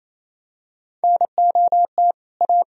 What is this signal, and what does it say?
Morse code, heard as a keyed tone, about 28 wpm, 700 Hz. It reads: NOT A
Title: Sentences from Top 1000 Words - Review Part 2 - 28wpm